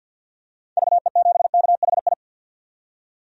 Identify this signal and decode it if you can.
Morse code, heard as a keyed tone, about 50 wpm, 700 Hz. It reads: VE7KHI
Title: DXCC-100 Call Signs - 6 Characters - 50wpm